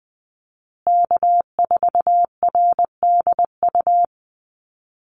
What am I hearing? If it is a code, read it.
Morse code, heard as a keyed tone, about 20 wpm, 700 Hz. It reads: K4RDU
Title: USA Call Signs Repeated 3x 20wpm